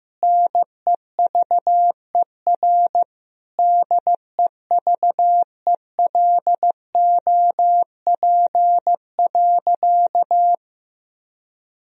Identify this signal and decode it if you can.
Morse code, heard as a keyed tone, about 15 wpm, 700 Hz. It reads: NEVER DEVELOP.
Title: Sentences from Top 1000 Words - Review Part 2 - 15wpm